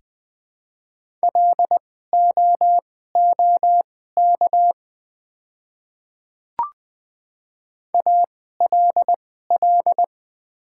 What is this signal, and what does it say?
Morse code, heard as a keyed tone, about 20 wpm, spaced as for 15 wpm, 700 Hz. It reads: LOOK ALL
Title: 2x Character Spacing - Top 200 Words - 20wpm